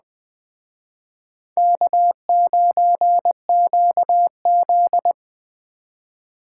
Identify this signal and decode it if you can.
Morse code, heard as a keyed tone, about 20 wpm, 700 Hz. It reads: K9QZ
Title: Speed Racing - USA 1x2 Call Signs 17wpm